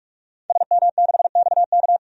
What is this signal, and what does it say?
Morse code, heard as a keyed tone, about 47 wpm, 700 Hz. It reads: SM6XK
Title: DXCC-100 Call Signs - 5 Characters - 45wpm